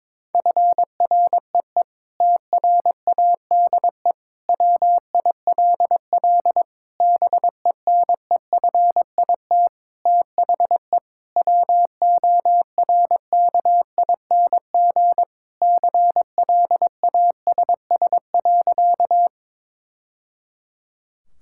Morse code, heard as a keyed tone, about 22 wpm, 700 Hz. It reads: FREE TRADE WILL BENEFIT THE WORKING CLASS.
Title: Sentences from Top 1000 Words - Review Part 6 - 22wpm